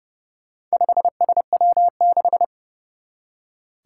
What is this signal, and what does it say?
Morse code, heard as a keyed tone, about 30 wpm, 690 Hz. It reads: 5SW6
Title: Four Challenging Letter-Number Combinations 30wpm